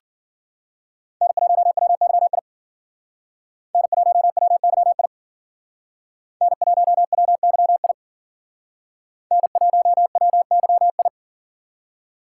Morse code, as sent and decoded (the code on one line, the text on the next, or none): -. .---- .-- -.-- .. / -. .---- .-- -.-- .. / -. .---- .-- -.-- .. / -. .---- .-- -.-- ..
N1WYI N1WYI N1WYI N1WYI